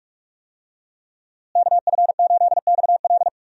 Morse code, heard as a keyed tone, about 43 wpm, 700 Hz. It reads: KF8XL
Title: USA Call Signs - 5 Characters - Repeated 2x - 45wpm